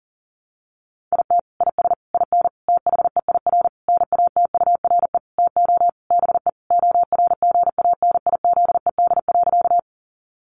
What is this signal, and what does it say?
Morse code, heard as a keyed tone, about 37 wpm, 700 Hz. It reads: IT IS IN THEIR NATURE TO BE ORGANIZED.
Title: Sentences from Top 1300 Words 40wpm